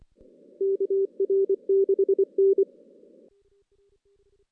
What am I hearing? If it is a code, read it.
Morse code, heard as a keyed tone, about 24 wpm, 410 Hz. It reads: KR6N
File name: 2016 JAN CW NAQP / 2016 Jan naqp k5ot - 0-6